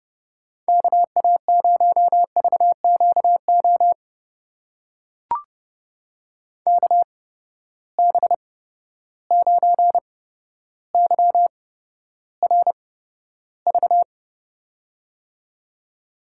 Morse code, heard as a keyed tone, about 30 wpm, spaced as for 13 wpm, 700 Hz. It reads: KA0VQO KB9YRV
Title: USA Call Signs - 6 Characters - 8x Character Spacing - 30wpm